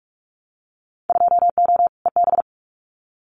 Morse code, heard as a keyed tone, about 46 wpm, 700 Hz. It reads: 2O EB
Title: Winter Field Day Exchanges Encore 45wpm